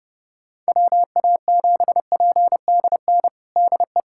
Morse code, heard as a keyed tone, about 30 wpm, 700 Hz. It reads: WA7PDN DE
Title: USA Callsign DE Callsign 30wpm